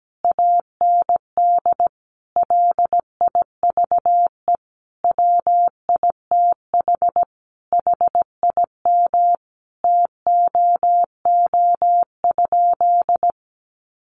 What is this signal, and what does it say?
Morse code, heard as a keyed tone, about 17 wpm, 700 Hz. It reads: AND LIVE WITH HIM TOO?
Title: Sentences from Top 500 Words 17wpm